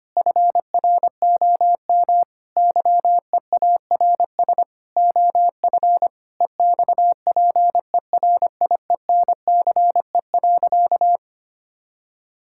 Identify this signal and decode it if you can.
Morse code, heard as a keyed tone, about 25 wpm, 700 Hz. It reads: FROM YEARS OF EXPERIENCE.
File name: practice / 25wpm / Difficult Sentences with Hints - 25wpm